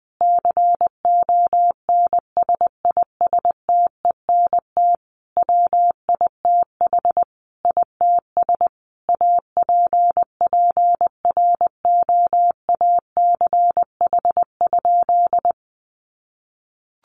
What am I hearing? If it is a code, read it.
Morse code, heard as a keyed tone, about 19 wpm, 700 Hz. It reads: CONSISTENT WITH ITS APPROACH?
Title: Sentences from Top 1500 Words 20wpm